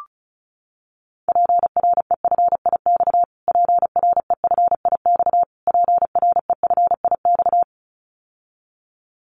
Morse code, heard as a keyed tone, about 34 wpm, 700 Hz. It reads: PREFIX PREFIX PREFIX
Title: Callsign Prefixes for Top 25 DXCC - Repeated 3x - 35wpm